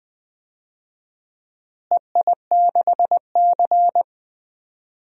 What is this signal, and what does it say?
Morse code, heard as a keyed tone, about 21 wpm, 700 Hz. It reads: EI6C
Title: DXCC-100 Call Signs - 4 Characters - Repeated 2x - 20wpm